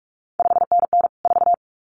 Morse code, heard as a keyed tone, about 46 wpm, 700 Hz. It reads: 5NN 4
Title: CQ WW DX Contest Exchange - 45wpm